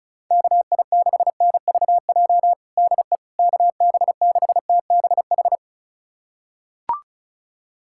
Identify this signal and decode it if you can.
Morse code, heard as a keyed tone, about 34 wpm, 700 Hz. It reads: KI6NVJ DE KB6TBH
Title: USA Callsign DE Callsign - 6 Characters - 35wpm